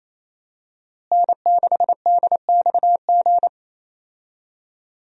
Morse code, heard as a keyed tone, about 27 wpm, 700 Hz. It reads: N6DXG
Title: USA Call Signs - 5 Characters - Repeated 3x - 28wpm